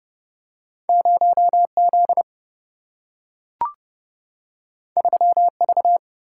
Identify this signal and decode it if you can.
Morse code, heard as a keyed tone, about 30 wpm, 700 Hz. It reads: 0Z 3V
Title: Single-Letter Single-Number 30wpm